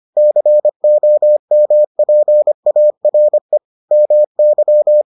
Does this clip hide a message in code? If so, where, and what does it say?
Yes: Morse code from 0.2 to 5.0 s, heard as a keyed tone, about 25 wpm, 600 Hz. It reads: COMPARE MY